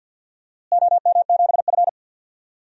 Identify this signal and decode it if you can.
Morse code, heard as a keyed tone, about 46 wpm, 700 Hz. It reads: OM7F
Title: DXCC-100 Call Signs - 4 Characters - Repeated 2x - 50wpm